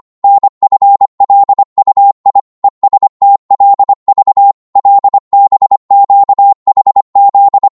Morse code, heard as a keyed tone, about 25 wpm, 800 Hz. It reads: NFLUI ESTLV LBQHZ
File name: Morse / ltr-25wpm